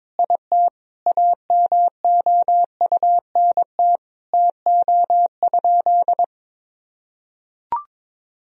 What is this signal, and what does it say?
Morse code, heard as a keyed tone, about 22 wpm, 700 Hz. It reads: IT AMOUNT TO?
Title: Sentences from Top 1000 Words - Review Part 4 - 22wpm